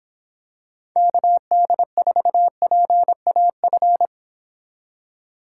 Morse code, heard as a keyed tone, about 26 wpm, 700 Hz. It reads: KD4PAF